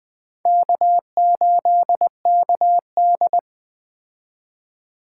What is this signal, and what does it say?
Morse code, heard as a keyed tone, about 20 wpm, 700 Hz. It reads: K8KD
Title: Speed Racing - USA Call Signs 17wpm